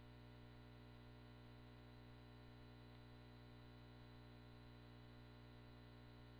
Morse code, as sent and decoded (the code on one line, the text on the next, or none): none